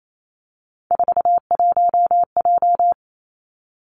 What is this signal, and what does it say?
Morse code, heard as a keyed tone, about 28 wpm, 700 Hz. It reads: 41J
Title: Three Challenging Letter-Number Combinations 28wpm